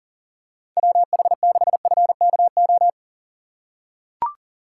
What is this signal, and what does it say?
Morse code, heard as a keyed tone, about 40 wpm, 700 Hz. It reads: WH6FKO
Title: USA Call Signs - 6 Characters - 6x Character Spacing - 40wpm